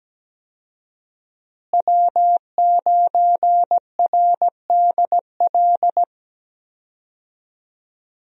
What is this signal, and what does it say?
Morse code, heard as a keyed tone, about 17 wpm, 700 Hz. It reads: W9RDL
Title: USA Call Signs - 5 Characters - Repeated 2x - 17wpm